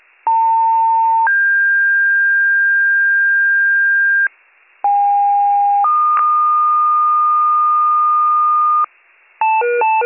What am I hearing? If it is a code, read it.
Two-tone page, heard as a one-second tone, then a three-second tone, about 900 then 1600 Hz, about 810 then 1220 Hz, with noise. It is heard from 0.3 to 8.8 s.